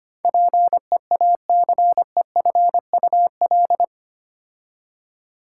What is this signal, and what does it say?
Morse code, heard as a keyed tone, about 25 wpm, 700 Hz. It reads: PEACEFUL
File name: 8 Letter Words - First 3 Letters Given - 25wpm